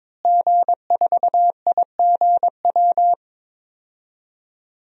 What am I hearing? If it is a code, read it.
Morse code, heard as a keyed tone, about 22 wpm, 700 Hz. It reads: G4IGW
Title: DXCC-100 Call Signs - 5 Characters - Repeated 2x - 22wpm